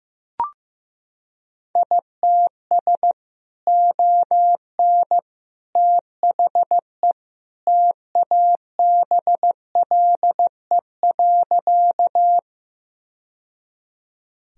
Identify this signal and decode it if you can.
Morse code, heard as a keyed tone, about 15 wpm, 700 Hz. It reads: ITS ON THE TABLE.
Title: Sentences from Top 1000 Words - Review Part 2 - 15wpm